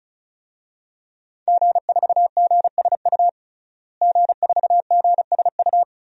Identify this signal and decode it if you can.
Morse code, heard as a keyed tone, about 36 wpm, 700 Hz. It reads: G4GSU G4GSU
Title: DXCC-100 Call Signs - 5 Characters - Repeated 3x - 35wpm